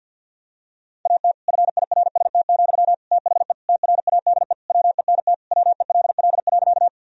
Morse code, heard as a keyed tone, about 46 wpm, 700 Hz. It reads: AT FIRST, THE TRADE WENT WELL.
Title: Sentences from Top 1000 Words - Review Part 4 - 50wpm